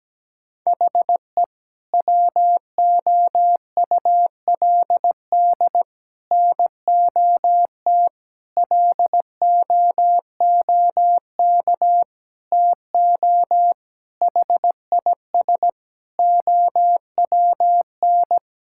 Morse code, heard as a keyed tone, about 17 wpm, 700 Hz. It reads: HE WOULD NOT LOOK TO HIS OWN
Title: Sentences from Top 1000 Words - Review Part 6 - 17wpm